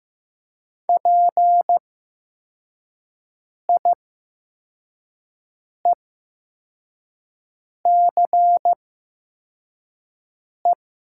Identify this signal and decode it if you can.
Morse code, heard as a keyed tone, about 15 wpm, spaced as for 4 wpm, 700 Hz. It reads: PIECE
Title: 8x Character Spacing - Top 500 Words - 15wpm